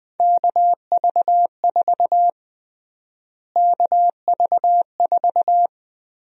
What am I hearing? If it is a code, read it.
Morse code, heard as a keyed tone, about 20 wpm, 700 Hz. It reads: KV4 KV4
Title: USA Call Sign Prefixes - Repeated 2x - 20wpm